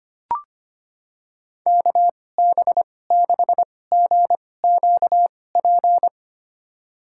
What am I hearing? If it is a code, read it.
Morse code, heard as a keyed tone, about 25 wpm, spaced as for 18 wpm, 700 Hz. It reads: KB6GQP
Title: USA Call Signs - 6 Characters - 2x Character Spacing - 25wpm